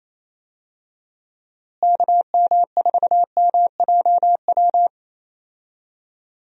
Morse code, heard as a keyed tone, about 28 wpm, 700 Hz. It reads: KM4MJW
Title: USA Call Signs - 6 Characters - Repeated 3x - 28wpm